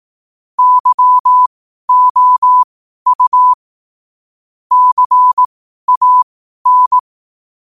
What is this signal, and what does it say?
Morse code, heard as a keyed tone, about 18 wpm, spaced as for 12 wpm, 1000 Hz. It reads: YOU CAN
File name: qotdfiles / 12wpm